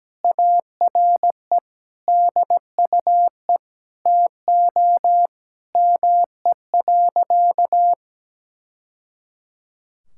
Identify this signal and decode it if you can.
Morse code, heard as a keyed tone, about 17 wpm, 700 Hz. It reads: ARE DUE TO ME.
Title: Sentences from Top 1000 Words - Review Part 6 - 17wpm